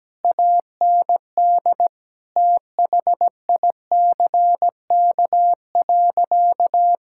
Morse code, heard as a keyed tone, about 17 wpm, 700 Hz. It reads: AND THICK.
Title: Sentences from Top 1600 Words 17wpm